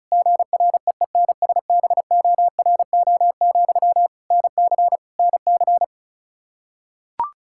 Morse code, heard as a keyed tone, about 34 wpm, 700 Hz. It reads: GREENSBORO, NC NC
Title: K1USN SST QSO Practice 35wpm